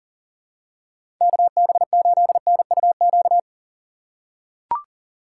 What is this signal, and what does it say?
Morse code, heard as a keyed tone, about 40 wpm, 700 Hz. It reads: KB8NUQ
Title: USA Call Signs - 6 Characters - 2x Character Spacing - 40wpm